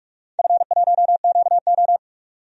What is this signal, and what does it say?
Morse code, heard as a keyed tone, about 46 wpm, 700 Hz. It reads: F1QO
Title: DXCC-100 Call Signs - 4 Characters - 45wpm